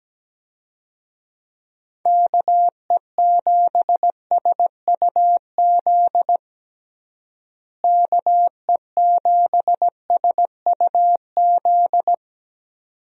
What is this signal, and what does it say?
Morse code, heard as a keyed tone, about 17 wpm, 700 Hz. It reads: KE7SUZ KE7SUZ
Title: USA Call Signs - 6 Characters - Repeated 2x - 17wpm